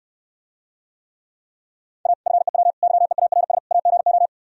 Morse code, heard as a keyed tone, about 67 wpm, 700 Hz. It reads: A FEW YEARS AGO